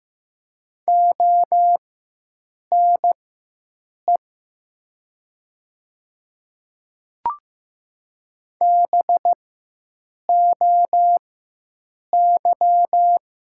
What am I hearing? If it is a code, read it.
Morse code, heard as a keyed tone, about 15 wpm, spaced as for 7 wpm, 700 Hz. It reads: ONE BOY
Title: 4x Character Spacing - Top 300 Words - 15wpm